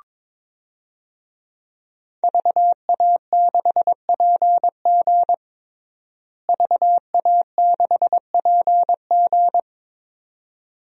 Morse code, heard as a keyed tone, about 22 wpm, 700 Hz. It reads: VA6PG VA6PG
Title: DXCC-100 Call Signs - 5 Characters - Repeated 2x - 22wpm